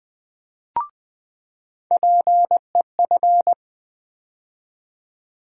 Morse code, heard as a keyed tone, about 20 wpm, 700 Hz. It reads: PEF
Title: USA Call Sign Postfixes - 3 Letters 20wpm